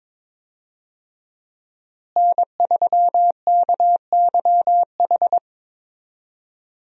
Morse code, heard as a keyed tone, about 22 wpm, 700 Hz. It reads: N3KYH
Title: USA Call Signs - 5 Characters - Repeated 2x - 22wpm